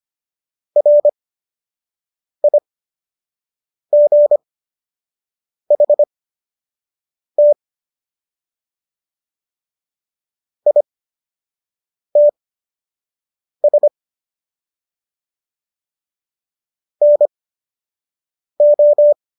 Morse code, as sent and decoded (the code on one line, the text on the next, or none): .-. .. --. .... - / .. - ... / -. ---
RIGHT ITS NO